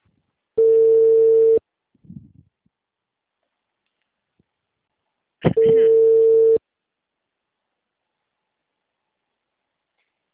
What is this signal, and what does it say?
Ringback tone (China, 450 Hz), with speech.